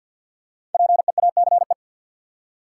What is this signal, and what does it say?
Morse code, heard as a keyed tone, about 52 wpm, 700 Hz. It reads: PEACE
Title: Five Letter Words - 50wpm